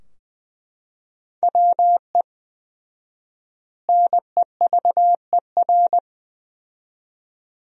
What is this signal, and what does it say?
Morse code, heard as a keyed tone, about 20 wpm, 700 Hz. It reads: WE NEVER
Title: 4x Word-Spacing - Sentences from Top 300 Words - 20wpm